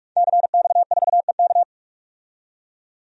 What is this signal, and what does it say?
Morse code, heard as a keyed tone, about 46 wpm, 700 Hz. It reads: CX4EK
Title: DXCC-100 Call Signs - 5 Characters - 45wpm